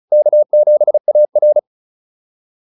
Morse code, heard as a keyed tone, about 34 wpm, 600 Hz. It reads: K7AR